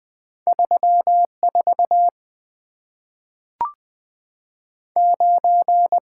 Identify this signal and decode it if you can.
Morse code, heard as a keyed tone, about 20 wpm, 700 Hz. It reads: 3 4 9
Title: Two Letter-Number Combinations - 20wpm